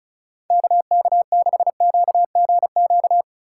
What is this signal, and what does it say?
Morse code, heard as a keyed tone, about 36 wpm, 700 Hz. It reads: KK6QGQ
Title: USA Call Signs - 6 Characters - 2x Character Spacing - 35wpm